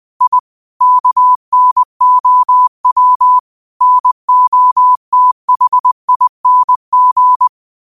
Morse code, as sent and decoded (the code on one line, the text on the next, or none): .. / -.- -. --- .-- / -. --- - .... .. -. --.
I KNOW NOTHING